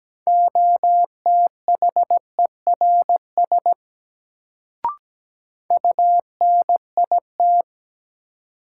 Morse code, heard as a keyed tone, about 17 wpm, 700 Hz. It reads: OTHERS UNIT
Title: Sentences from Top 1000 Words - Review Part 2 - New Word Drill - 17wpm